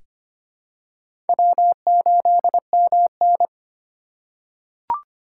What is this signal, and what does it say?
Morse code, heard as a keyed tone, about 25 wpm, 700 Hz. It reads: W8MN
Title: USA Call Signs - 4 Characters - 2x Character Spacing - 25wpm